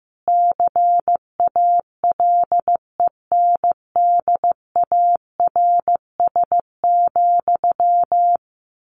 Morse code, heard as a keyed tone, about 15 wpm, 700 Hz. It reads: CALENDARS,
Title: A Sound Of Thunder 15wpm